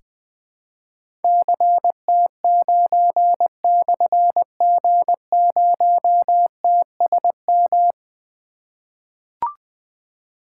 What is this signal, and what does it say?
Morse code, heard as a keyed tone, about 20 wpm, 700 Hz. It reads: CT9/G0TSM
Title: Difficult Call Signs 20wpm